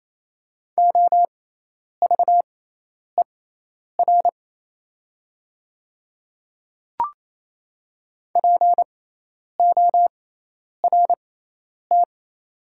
Morse code, heard as a keyed tone, about 28 wpm, spaced as for 10 wpm, 700 Hz. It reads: OVER PORT
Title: 6x Character Spacing - Top 200 Words - 28wpm